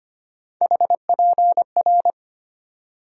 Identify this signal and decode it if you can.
Morse code, heard as a keyed tone, about 25 wpm, 700 Hz. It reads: HPR
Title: USA Call Sign Postfixes - 3 Letters 25wpm